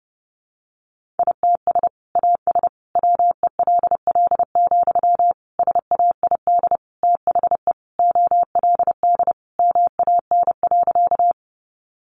Morse code, heard as a keyed tone, about 30 wpm, 700 Hz. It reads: ITS AS WELL, SAID THE OLD MAN.
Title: Sentences from Top 600 Words 30wpm